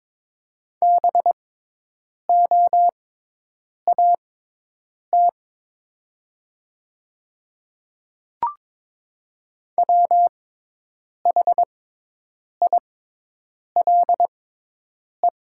Morse code, heard as a keyed tone, about 22 wpm, spaced as for 8 wpm, 700 Hz. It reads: BOAT WHILE
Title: 6x Character Spacing - Top 500 Words - 22wpm